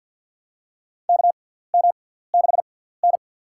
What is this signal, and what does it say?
Morse code, heard as a keyed tone, about 52 wpm, spaced as for 17 wpm, 690 Hz. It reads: KM6N
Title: USA Call Signs - 6x Character Spacing - 50wpm